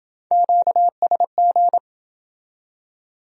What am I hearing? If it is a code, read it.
Morse code, heard as a keyed tone, about 28 wpm, 700 Hz. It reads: QSG